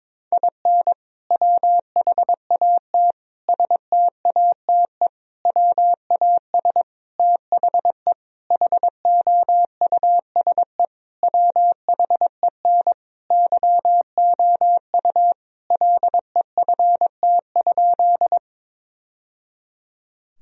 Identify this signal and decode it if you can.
Morse code, heard as a keyed tone, about 21 wpm, 700 Hz. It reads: IN WHAT STATE WAS THE HOUSE WHEN YOU LEFT?
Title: Sentences from Top 1000 Words - Review Part 2 - 22wpm